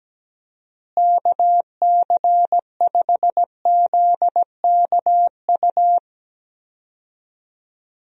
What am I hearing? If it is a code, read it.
Morse code, heard as a keyed tone, about 17 wpm, 700 Hz. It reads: KC5ZKU
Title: USA Call Signs - 6 Characters - Repeated 3x - 17wpm